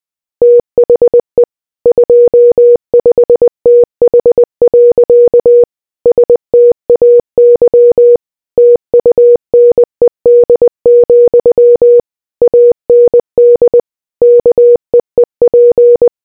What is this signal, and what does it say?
Morse code, heard as a keyed tone, about 20 wpm, 480 Hz. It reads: THE 25TH. STAY TUNED, AND KEEP